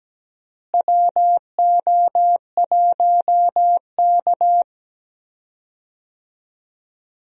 Morse code, heard as a keyed tone, about 17 wpm, 700 Hz. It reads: WO1K
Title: USA Call Signs - 4 Characters - Repeated 2x - 17wpm